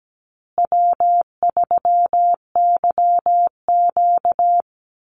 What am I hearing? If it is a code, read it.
Morse code, heard as a keyed tone, about 17 wpm, 700 Hz. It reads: W3YQ
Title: Call Signs Repeated 2x - 17wpm